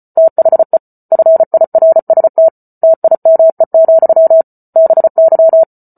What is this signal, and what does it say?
Morse code, heard as a keyed tone, about 34 wpm, 650 Hz. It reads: THE FIRST TIME, BY
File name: Book Two / 08-Chapter 8 - 34 WPM